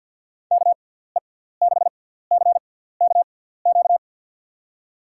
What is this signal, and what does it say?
Morse code, heard as a keyed tone, about 51 wpm, spaced as for 17 wpm, 700 Hz. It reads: KE6CKQ
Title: USA Call Signs - 6 Characters - 6x Character Spacing - 50wpm